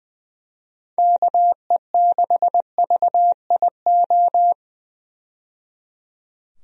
Morse code, heard as a keyed tone, about 20 wpm, 700 Hz. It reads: KE6VIO
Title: USA Call Signs - 6 Characters - 20wpm